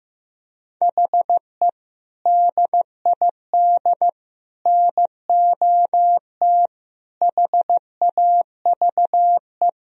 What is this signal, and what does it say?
Morse code, heard as a keyed tone, about 15 wpm, 700 Hz. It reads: HE DID NOT HAVE
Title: Speed-Racing - Sets of 5 Words 15wpm